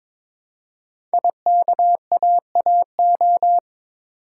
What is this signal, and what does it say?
Morse code, heard as a keyed tone, about 22 wpm, 700 Hz. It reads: IKAAO
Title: Code Groups - 5 Letters - 22wpm